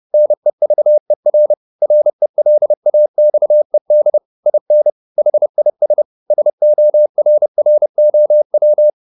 Morse code, heard as a keyed tone, about 30 wpm, 600 Hz. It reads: NEVER RELAXED IN HIS SORROW